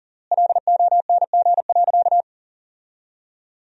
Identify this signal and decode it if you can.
Morse code, heard as a keyed tone, about 40 wpm, 700 Hz. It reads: LONG.
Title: 3x Word-Spacing - Sentences from Top 100 Words - 40wpm